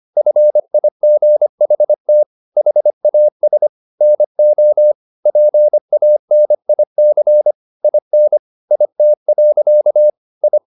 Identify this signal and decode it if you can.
Morse code, heard as a keyed tone, about 25 wpm, 600 Hz. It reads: FIGHT HAS NO PANIC IN IT. I
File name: the old man and the sea part 1 25wpm